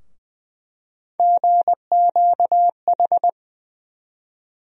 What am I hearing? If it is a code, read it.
Morse code, heard as a keyed tone, about 20 wpm, 700 Hz. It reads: GQH
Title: USA Call Sign Postfixes - 3 Letters 20wpm